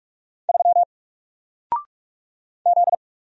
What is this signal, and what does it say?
Morse code, heard as a keyed tone, about 44 wpm, 700 Hz. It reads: 3 Z
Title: Single Letter-Number Alt 45wpm